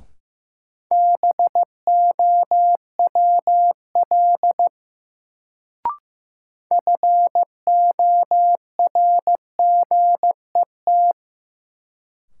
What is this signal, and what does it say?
Morse code, heard as a keyed tone, about 15 wpm, 700 Hz. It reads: BOWL FORGET
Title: Sentences from Top 1200 Words - New Word Drill - 15wpm